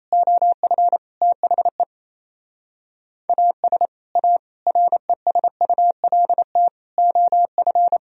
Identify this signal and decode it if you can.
Morse code, heard as a keyed tone, about 37 wpm, 700 Hz. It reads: OF THE AS A RESULT OF